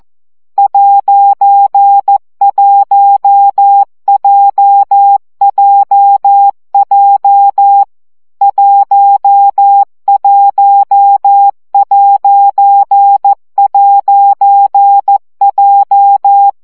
Morse code, heard as a keyed tone, about 14 wpm, 800 Hz. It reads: '1JJJ 11''J
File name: CW gyakorlas j 1 aposztrof 12 baud